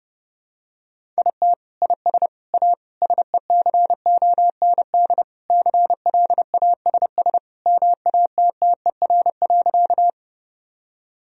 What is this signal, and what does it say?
Morse code, heard as a keyed tone, about 30 wpm, 700 Hz. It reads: IT IS A SECOND CLASS MATTER.
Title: Sentences from Top 1000 Words - Review Part 2 - 30wpm